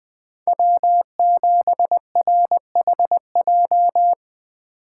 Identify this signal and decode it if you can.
Morse code, heard as a keyed tone, about 21 wpm, 700 Hz. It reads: W7RHJ
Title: Speed Racing - USA 1x3 Call Signs 17wpm